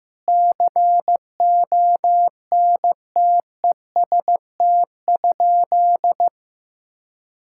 Morse code, heard as a keyed tone, about 15 wpm, 700 Hz. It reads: CONTEST?
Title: Sentences from Top 1600 Words 15wpm